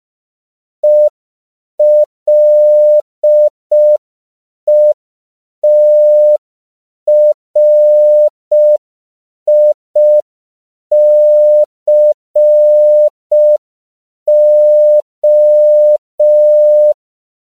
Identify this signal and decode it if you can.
Morse code, heard as a keyed tone, about 5 wpm, 600 Hz. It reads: ELETRICO
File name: CW / 5PPM